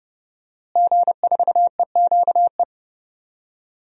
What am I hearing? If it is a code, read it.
Morse code, heard as a keyed tone, about 30 wpm, 700 Hz. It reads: G4EQE
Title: DXCC-100 Call Signs - 5 Characters - Repeated 2x - 30wpm